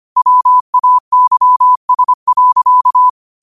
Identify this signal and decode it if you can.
Morse code, heard as a keyed tone, about 25 wpm, 1000 Hz. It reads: WAYS.
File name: qotdfiles / 25wpm